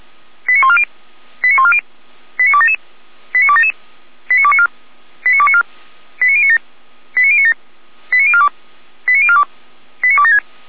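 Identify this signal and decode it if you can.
Five-tone selective call (ZVEI), heard as quick runs of five tones, tones of about 70 ms. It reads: 89140 89140 89277 89277 89294 89294 89097 89097 89042 89042 89276
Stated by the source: Received by SDR; ZVEI five-tone call, Feuerwehr Orlamünde fire service alerting.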